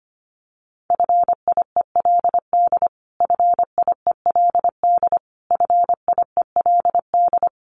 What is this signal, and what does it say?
Morse code, heard as a keyed tone, about 25 wpm, 700 Hz. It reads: FIELD FIELD FIELD
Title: Top 500 Words Repeated 25wpm